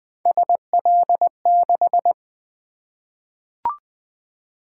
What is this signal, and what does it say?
Morse code, heard as a keyed tone, about 20 wpm, 700 Hz. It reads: SL6
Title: Three Letter-Number Combinations - 20wpm